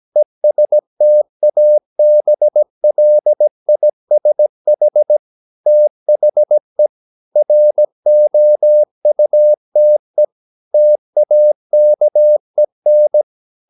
Morse code, heard as a keyed tone, about 17 wpm, 600 Hz. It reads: ESTABLISH THE ROUTE TAKEN
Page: Romanen Never af Ken Follet på engelsk.